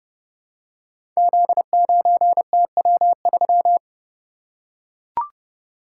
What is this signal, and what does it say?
Morse code, heard as a keyed tone, about 30 wpm, 700 Hz. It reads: Z9TW3
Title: Code Groups - 5 Letter:Numbers - 30wpm